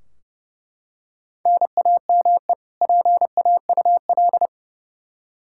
Morse code, heard as a keyed tone, about 30 wpm, 700 Hz. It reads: NAME PAUL
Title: Practice QSO - Exchange 1 - Beginning - 30wpm